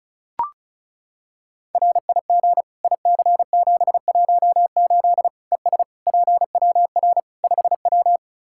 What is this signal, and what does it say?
Morse code, heard as a keyed tone, about 36 wpm, 700 Hz. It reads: RIG IC718 ES PWR 5W
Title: Practice QSO - Exchange 2 - Info - 35wpm